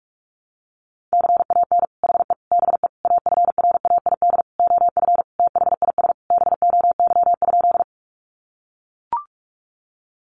Morse code, heard as a keyed tone, about 44 wpm, 700 Hz. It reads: CAN HE BE AFRAID OF THIS BOY?
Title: Sentences from Top 1500 Words 45wpm